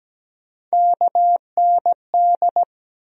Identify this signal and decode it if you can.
Morse code, heard as a keyed tone, about 17 wpm, 700 Hz. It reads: KND
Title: USA Call Sign Postfixes - 3 Letters 17wpm